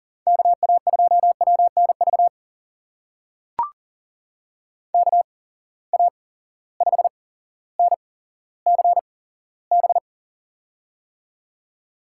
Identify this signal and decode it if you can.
Morse code, heard as a keyed tone, about 40 wpm, spaced as for 17 wpm, 700 Hz. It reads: KA2WNV KA5NCB
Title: USA Call Signs - 6 Characters - 8x Character Spacing - 40wpm